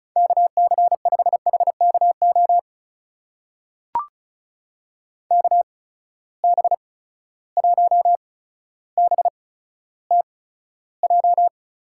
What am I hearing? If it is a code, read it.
Morse code, heard as a keyed tone, about 36 wpm, spaced as for 15 wpm, 700 Hz. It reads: KC5HKO KB1BTJ
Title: USA Call Signs - 6 Characters - 8x Character Spacing - 35wpm